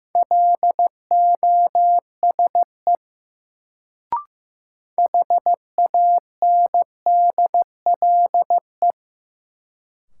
Morse code, heard as a keyed tone, about 15 wpm, 700 Hz. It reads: LOSE HANDLE
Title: Sentences from Top 1100 Words - New Word Drill Alt - 15wpm